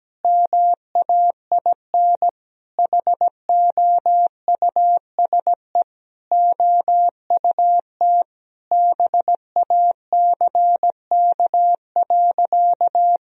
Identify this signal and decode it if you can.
Morse code, heard as a keyed tone, about 17 wpm, 700 Hz. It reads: MAIN HOUSE OUT BACK.
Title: Sentences from Top 1000 Words - Review Part 2 - 17wpm